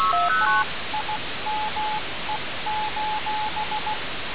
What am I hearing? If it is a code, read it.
Morse code, heard as a keyed tone, about 16 wpm, 860 Hz, with noise. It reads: TIME:
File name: radiocap / REC z